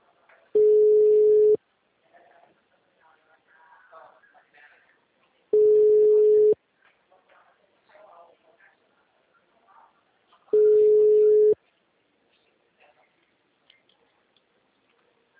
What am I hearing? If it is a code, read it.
Ringback tone (Europe, 425 Hz), with speech.
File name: out- -2004- - - .3884